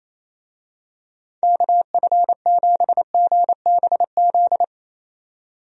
Morse code, heard as a keyed tone, about 28 wpm, 700 Hz. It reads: KF7GBZ
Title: USA Call Signs - 6 Characters - 28wpm